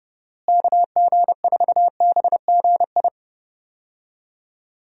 Morse code, heard as a keyed tone, about 30 wpm, 700 Hz. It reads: KG4BGI
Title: USA Call Signs - 6 Characters - Repeated 2x - 30wpm